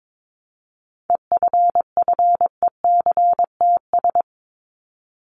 Morse code, heard as a keyed tone, about 22 wpm, 700 Hz. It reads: EFFECTS
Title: ES:S Suffix - Top 100 Words with Base Words 22wpm